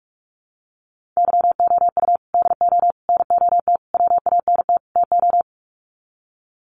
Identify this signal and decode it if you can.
Morse code, heard as a keyed tone, about 44 wpm, 700 Hz. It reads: YOU DO NOT WANT TO
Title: Sets of 5 Words 45wpm